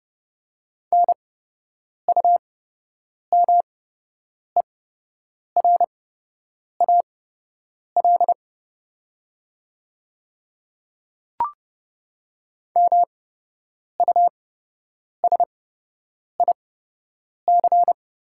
Morse code, heard as a keyed tone, about 30 wpm, spaced as for 8 wpm, 700 Hz. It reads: NUMERAL MUSIC
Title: 8x Character Spacing - Top 500 Words - 30wpm